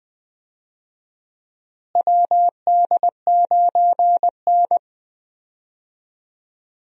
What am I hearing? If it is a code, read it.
Morse code, heard as a keyed tone, about 20 wpm, 700 Hz. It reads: WD9N
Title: USA Call Signs - 4 Characters - Repeated 3x - 20wpm